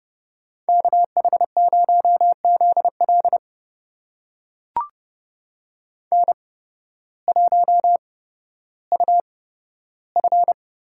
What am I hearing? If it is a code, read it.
Morse code, heard as a keyed tone, about 30 wpm, spaced as for 14 wpm, 700 Hz. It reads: KH0ZL N1UF
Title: USA Call Signs - 8x Character Spacing - 30wpm